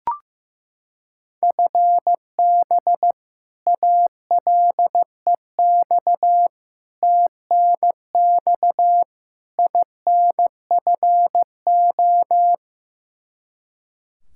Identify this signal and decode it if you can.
Morse code, heard as a keyed tone, about 15 wpm, 700 Hz. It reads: FB ALEX TNX INFO
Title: Practice QSO - Exchange 3 - More Info - 15wpm